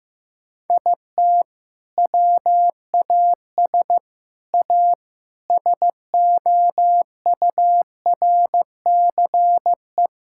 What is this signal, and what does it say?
Morse code, heard as a keyed tone, about 16 wpm, 700 Hz. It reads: IT WAS A SOURCE
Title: Sentences from Top 1000 Words - Review Part 5 - 15wpm